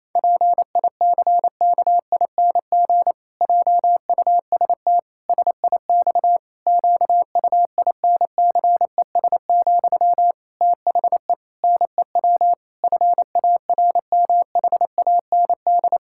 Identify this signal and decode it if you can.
Morse code, heard as a keyed tone, about 28 wpm, 700 Hz. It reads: PICKING JUST SIX QUINCES, THE NEW FARMHAND
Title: Pangram Challenge v2 - 28wpm